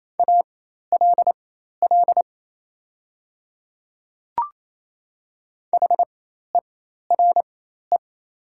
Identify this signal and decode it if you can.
Morse code, heard as a keyed tone, about 28 wpm, spaced as for 13 wpm, 690 Hz. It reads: ALL HERE
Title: 4x Character Spacing - Top 200 Words - 28wpm